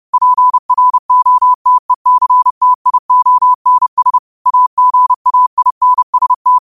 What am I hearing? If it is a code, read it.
Morse code, heard as a keyed tone, about 30 wpm, 1000 Hz. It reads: PROTECTIONS AGAINST
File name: qotdfiles / 30wpm